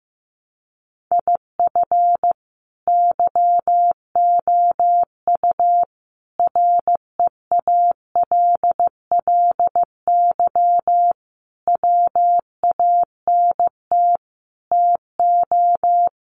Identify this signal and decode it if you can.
Morse code, heard as a keyed tone, about 15 wpm, 700 Hz. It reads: IF YOU REALLY WANT TO
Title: Sets of 5 Words Encore2 15wpm